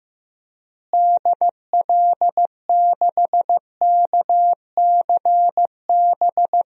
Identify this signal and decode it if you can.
Morse code, heard as a keyed tone, about 16 wpm, 700 Hz. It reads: DL6KCB
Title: DXCC-100 Call Signs - 6 Characters - Repeated 3x - 15wpm